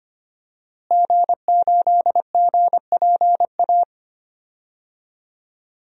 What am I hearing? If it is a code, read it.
Morse code, heard as a keyed tone, about 25 wpm, 700 Hz. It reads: G8GPA
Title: DXCC-100 Call Signs - 5 Characters - Repeated 2x - 25wpm